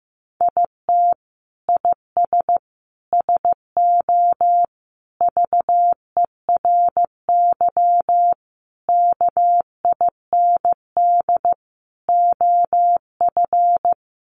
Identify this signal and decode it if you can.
Morse code, heard as a keyed tone, about 15 wpm, 700 Hz. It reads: IT IS SO VERY KIND OF
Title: Sentences from Top 300 Words 15wpm